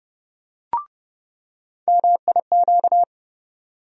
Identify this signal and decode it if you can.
Morse code, heard as a keyed tone, about 30 wpm, 700 Hz. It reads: MIQ
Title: USA Call Sign Postfixes - 3 Letters 30wpm